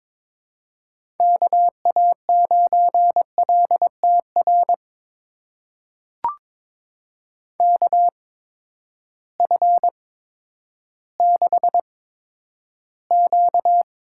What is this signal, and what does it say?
Morse code, heard as a keyed tone, about 22 wpm, spaced as for 11 wpm, 700 Hz. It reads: KA9LTR KF6Q
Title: USA Call Signs - 6 Characters - 8x Character Spacing - 22wpm